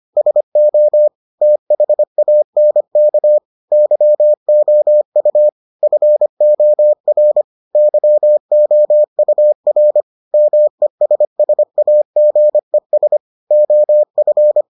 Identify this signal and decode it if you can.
Morse code, heard as a keyed tone, about 25 wpm, 600 Hz. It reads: SO THANK YOU FOR YOUR MESSAGES OF